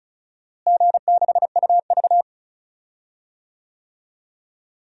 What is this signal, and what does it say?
Morse code, heard as a keyed tone, about 36 wpm, 700 Hz. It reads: G6UV
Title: DXCC-100 Call Signs - 4 Characters - Repeated 3x - 35wpm